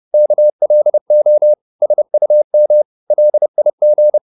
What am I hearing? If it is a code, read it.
Morse code, heard as a keyed tone, about 30 wpm, 600 Hz. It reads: KLO SUM LIG
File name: Korte ord DK 30wpm